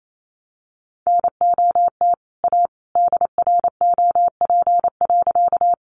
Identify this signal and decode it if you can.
Morse code, heard as a keyed tone, about 28 wpm, 700 Hz. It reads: NOT A DROP.
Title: Sentences from Top 1400 Words 28wpm